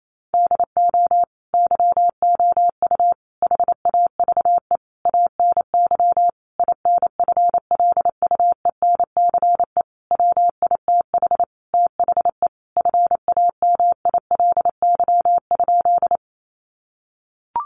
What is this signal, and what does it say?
Morse code, heard as a keyed tone, about 28 wpm, 700 Hz. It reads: DO YOU HAVE ANY INFLUENCE WITH THE FAMILY?
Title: Sentences from Top 1100 Words 28wpm